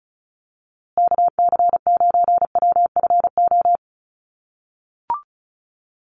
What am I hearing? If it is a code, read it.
Morse code, heard as a keyed tone, about 34 wpm, 700 Hz. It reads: KC9WFO
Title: USA Call Signs Repeated 3x 35wpm